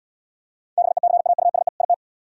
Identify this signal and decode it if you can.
Morse code, heard as a keyed tone, about 74 wpm, 700 Hz. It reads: BECAUSE IT